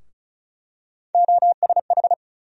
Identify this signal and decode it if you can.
Morse code, heard as a keyed tone, about 36 wpm, 700 Hz. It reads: OSH
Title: USA Call Sign Postfixes - 3 Letters 35wpm